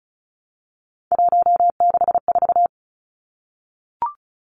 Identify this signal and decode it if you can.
Morse code, heard as a keyed tone, about 34 wpm, 700 Hz. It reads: 164
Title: Numbers - 3-Digit 35wpm